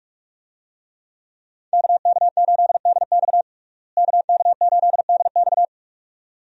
Morse code, heard as a keyed tone, about 43 wpm, 700 Hz. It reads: KK8DX KK8DX
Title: USA Call Signs - 5 Characters - Repeated 2x - 45wpm